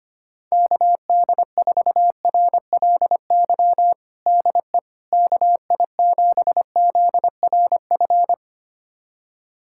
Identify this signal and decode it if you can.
Morse code, heard as a keyed tone, about 25 wpm, 700 Hz. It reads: KD4RLY DE KI7ZRF
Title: USA Callsign DE Callsign 25wpm